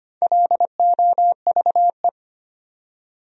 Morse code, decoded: LOVE

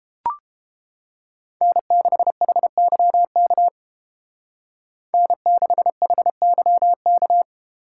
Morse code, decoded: N6HYK N6HYK

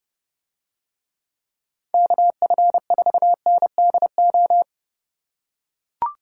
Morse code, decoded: KF4NDO

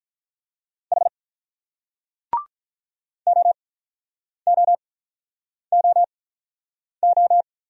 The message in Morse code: H OOOO

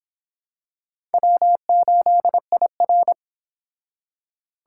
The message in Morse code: W8IR

